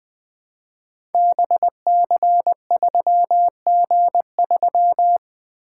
Morse code, heard as a keyed tone, about 21 wpm, 700 Hz. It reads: BC3G3
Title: Code Groups - 5 Letter:Numbers - 20wpm